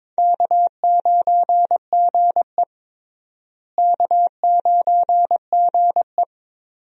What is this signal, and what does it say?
Morse code, heard as a keyed tone, about 22 wpm, 700 Hz. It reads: K9GE K9GE
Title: USA Call Signs - 4 Characters - Repeated 3x - 22wpm